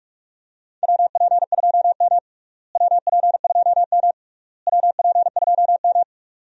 Morse code, heard as a keyed tone, about 45 wpm, 700 Hz. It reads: WP2M WP2M WP2M
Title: USA Call Signs - 4 Characters - Repeated 3x - 45wpm